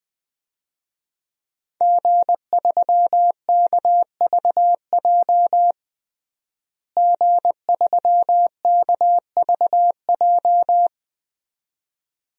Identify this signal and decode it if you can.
Morse code, heard as a keyed tone, about 21 wpm, 700 Hz. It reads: G3KVJ G3KVJ
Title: DXCC-100 Call Signs - 5 Characters - Repeated 2x - 20wpm